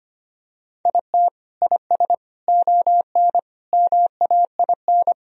Morse code, heard as a keyed tone, about 25 wpm, 700 Hz. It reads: IT IS ON MAIN
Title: Sentences from Top 1000 Words - Review Part 2 - 25wpm